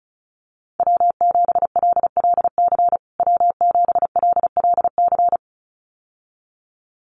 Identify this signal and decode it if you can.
Morse code, heard as a keyed tone, about 34 wpm, 700 Hz. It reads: W7LLC W7LLC
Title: USA Call Signs Repeated 2x 35wpm